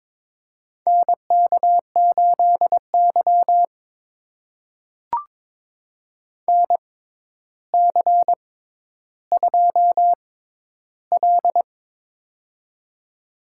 Morse code, heard as a keyed tone, about 23 wpm, spaced as for 11 wpm, 700 Hz. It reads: NK8Y NC2L
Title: USA Call Signs - 4 Characters - 6x Character Spacing - 22wpm